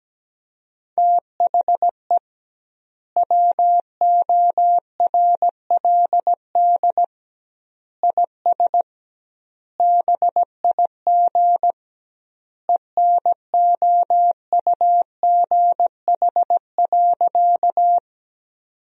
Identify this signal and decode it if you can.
Morse code, heard as a keyed tone, about 17 wpm, 700 Hz. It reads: THE WORLD IS BIG ENOUGH.
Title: 2x Word-Spacing - Sentences from Top 400 Words - 17wpm